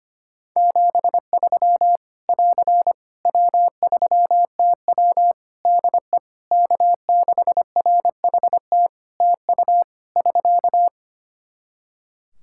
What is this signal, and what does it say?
Morse code, heard as a keyed tone, about 25 wpm, 700 Hz. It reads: 73 <AR> W3TW DE K6RHT TU <SK>
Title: Practice QSO - Exchange 4 - Ending - 25wpm